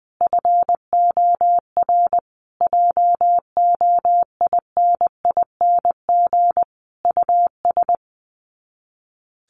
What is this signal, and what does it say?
Morse code, heard as a keyed tone, about 20 wpm, 700 Hz. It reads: FOR JOINING US
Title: Sets of 5 Words 20wpm